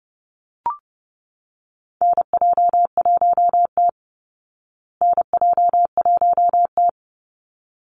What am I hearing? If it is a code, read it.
Morse code, heard as a keyed tone, about 30 wpm, 700 Hz. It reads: NJ1T NJ1T
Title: Call Signs Repeated 3x - 30wpm